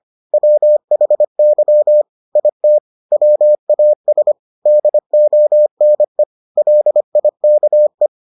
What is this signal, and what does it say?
Morse code, heard as a keyed tone, about 25 wpm, 600 Hz. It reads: WHY IT WAS DONE LIKE